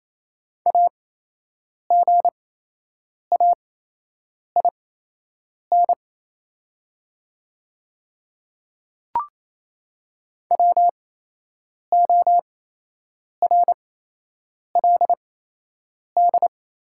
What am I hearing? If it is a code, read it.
Morse code, heard as a keyed tone, about 28 wpm, spaced as for 8 wpm, 700 Hz. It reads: AGAIN WORLD
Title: 8x Character Spacing - Top 200 Words - 28wpm